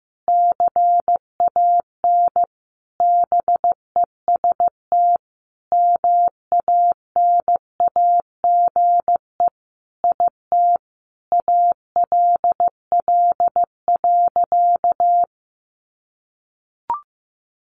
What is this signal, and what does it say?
Morse code, heard as a keyed tone, about 15 wpm, 700 Hz. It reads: CAN BEST MANAGE IT ALL.
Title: Sentences from Top 1400 Words 15wpm